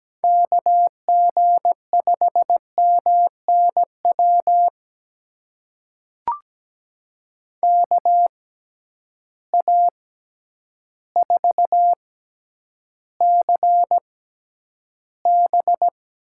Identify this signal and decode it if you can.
Morse code, heard as a keyed tone, about 17 wpm, spaced as for 9 wpm, 700 Hz. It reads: KG5MNW KA4CB
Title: USA Call Signs - 6 Characters - 6x Character Spacing - 17wpm